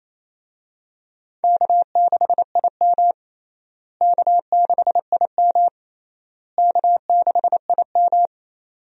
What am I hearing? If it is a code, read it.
Morse code, heard as a keyed tone, about 28 wpm, 700 Hz. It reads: K6IM K6IM K6IM
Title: USA Call Signs - 4 Characters - Repeated 3x - 28wpm